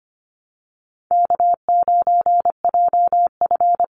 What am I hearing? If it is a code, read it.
Morse code, heard as a keyed tone, about 25 wpm, 700 Hz. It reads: K9JF
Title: Easy Call Signs Repeated 3x - 25wpm